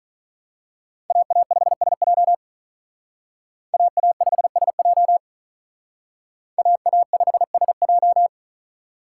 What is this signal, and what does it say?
Morse code, heard as a keyed tone, about 40 wpm, 700 Hz. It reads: AA5SJ AA5SJ AA5SJ
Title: Speed Racing - USA Call Signs 35wpm